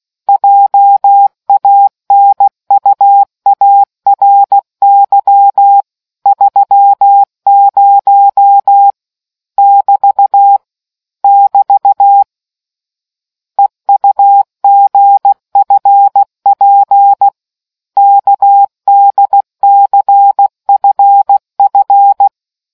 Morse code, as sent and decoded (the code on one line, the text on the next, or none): .--- .- -. ..- .- .-. -.-- / ...-- ----- / -...- / -...- / . ..- --. ..-. .--. / -.- -.. -.-. ..-. ..-.
JANUARY 30 <BT> <BT> EUGFP KDCFF